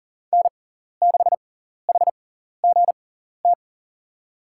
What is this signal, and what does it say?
Morse code, heard as a keyed tone, about 37 wpm, spaced as for 14 wpm, 690 Hz. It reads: N6HGT
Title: USA Call Signs - 6x Character Spacing - 40wpm